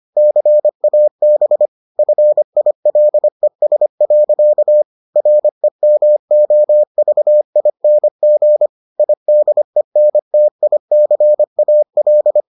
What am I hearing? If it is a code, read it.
Morse code, heard as a keyed tone, about 25 wpm, 600 Hz. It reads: CAB FILES. REMOVING IDENTICAL